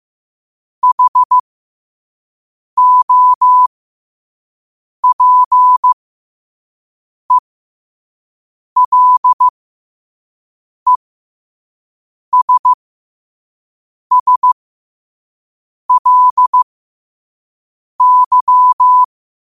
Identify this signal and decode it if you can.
Morse code, heard as a keyed tone, about 15 wpm, spaced as for 5 wpm, 1000 Hz. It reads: HOPELESSLY
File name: qotdfiles / 5wpm